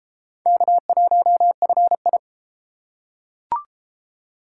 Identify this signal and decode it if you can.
Morse code, heard as a keyed tone, about 30 wpm, 700 Hz. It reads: K1FI